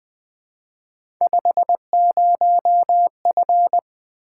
Morse code, heard as a keyed tone, about 20 wpm, 700 Hz. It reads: 50F
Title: Three Letter-Number Combinations - 20wpm